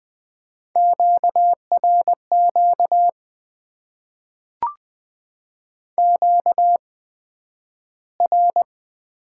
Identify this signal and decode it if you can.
Morse code, heard as a keyed tone, about 20 wpm, spaced as for 11 wpm, 700 Hz. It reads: QRQ QR
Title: Top 90 QSO Elements - 8x Character Spacing - 20wpm